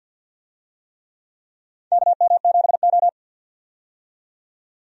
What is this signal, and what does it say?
Morse code, heard as a keyed tone, about 52 wpm, 700 Hz. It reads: KM7O
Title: USA Call Signs - 4 Characters - 50wpm